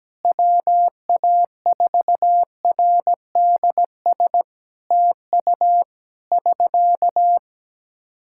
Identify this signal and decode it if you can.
Morse code, heard as a keyed tone, about 17 wpm, 700 Hz. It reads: WA4RDS TU <SK>
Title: Practice QSO - Exchange 4 - Ending - 17wpm